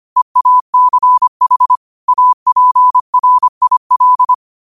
Morse code, heard as a keyed tone, about 25 wpm, 1000 Hz. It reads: EACH APRIL
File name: qotdfiles / 25wpm